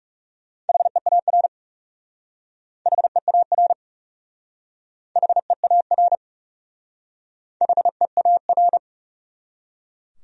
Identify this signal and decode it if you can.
Morse code, heard as a keyed tone, about 40 wpm, 700 Hz. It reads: HEAR HEAR HEAR HEAR